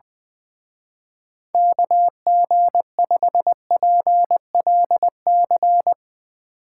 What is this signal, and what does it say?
Morse code, heard as a keyed tone, about 22 wpm, 700 Hz. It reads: KG5PLC
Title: Speed Racing - USA 2x3 Call Signs 20wpm